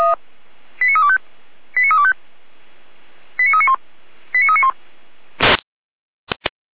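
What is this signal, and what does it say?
Five-tone selective call (ZVEI), heard as quick runs of five tones, tones of about 70 ms. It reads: 89325 89325 89391 89391
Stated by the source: ZVEI five-tone selective call received by SDR.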